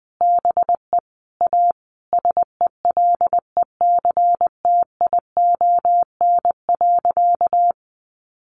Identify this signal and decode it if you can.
Morse code, heard as a keyed tone, about 19 wpm, 700 Hz. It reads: BE A SELECTION.
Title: Sentences from Top 1400 Words 20wpm